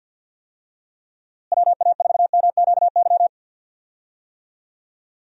Morse code, heard as a keyed tone, about 45 wpm, 700 Hz. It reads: WA4MQY
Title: USA Call Signs - 6 Characters - Repeated 3x - 50wpm